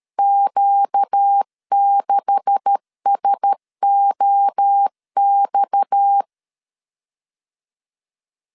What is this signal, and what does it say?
Morse code, heard as a keyed tone, about 13 wpm, 800 Hz. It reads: Q6SOX
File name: CW / DET / 1 9 1500C S C 12wpm 3811